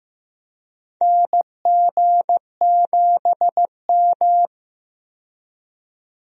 Morse code, heard as a keyed tone, about 15 wpm, 700 Hz. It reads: NG7M
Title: USA Call Signs - 4 Characters - 15wpm